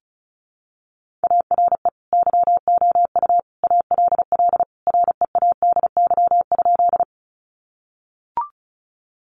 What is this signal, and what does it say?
Morse code, heard as a keyed tone, about 34 wpm, 700 Hz. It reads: ARE YOU ALL READY?
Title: Sentences from Top 400 Words 35wpm